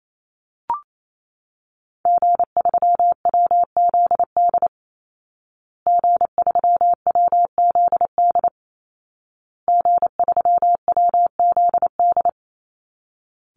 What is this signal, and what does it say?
Morse code, heard as a keyed tone, about 28 wpm, 700 Hz. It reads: G3WZD G3WZD G3WZD
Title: Call Signs Repeated 2x - 28wpm